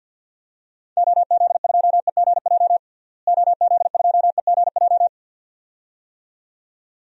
Morse code, heard as a keyed tone, about 50 wpm, 700 Hz. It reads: OZ2EGJ OZ2EGJ
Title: DXCC-100 Call Signs - 6 Characters - Repeated 2x - 50wpm